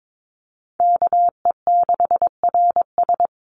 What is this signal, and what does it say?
Morse code, heard as a keyed tone, about 22 wpm, 700 Hz. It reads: KE6RS
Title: NA SOTA Chaser Callsigns 22wpm